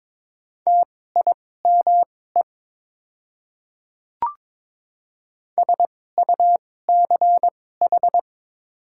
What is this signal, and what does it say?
Morse code, heard as a keyed tone, about 22 wpm, spaced as for 16 wpm, 700 Hz. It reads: TIME SUCH
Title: 2x Character Spacing - Top 200 Words - 22wpm